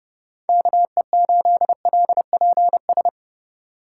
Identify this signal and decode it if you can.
Morse code, heard as a keyed tone, about 28 wpm, 700 Hz. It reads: KE8LPS